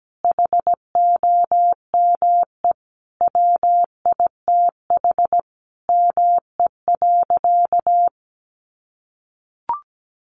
Morse code, heard as a keyed tone, about 17 wpm, 700 Hz. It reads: HOME WITH ME.
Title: Sentences from Top 800 Words 17wpm